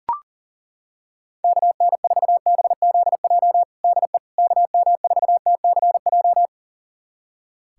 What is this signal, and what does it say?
Morse code, heard as a keyed tone, about 40 wpm, 700 Hz. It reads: KN4BZJ DE KM4TCJ
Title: USA Callsign DE Callsign - 6 Characters - 40wpm